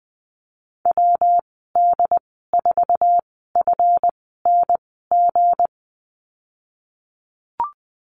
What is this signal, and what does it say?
Morse code, heard as a keyed tone, about 20 wpm, spaced as for 15 wpm, 700 Hz. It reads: WD4FNG
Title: 167 Lesson 27g - Number 7 - Call Signs 20wpm